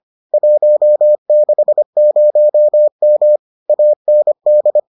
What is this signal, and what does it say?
Morse code, heard as a keyed tone, about 25 wpm, 600 Hz. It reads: 160M AND